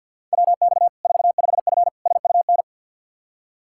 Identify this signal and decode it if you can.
Morse code, heard as a keyed tone, about 52 wpm, 700 Hz. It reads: WX 45F SUN
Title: K1USN SST QSO Practice 50wpm